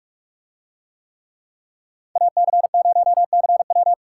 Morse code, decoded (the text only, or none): AC0CW